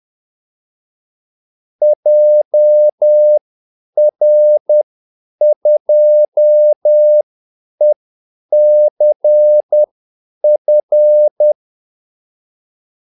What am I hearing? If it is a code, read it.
Morse code, heard as a keyed tone, about 10 wpm, spaced as for 8 wpm, 600 Hz. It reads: JR2ECF
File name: Callsigns-10-8wpm